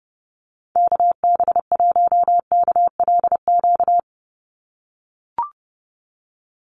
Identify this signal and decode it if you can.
Morse code, heard as a keyed tone, about 30 wpm, 700 Hz. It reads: KB1KLQ
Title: USA Call Signs 30wpm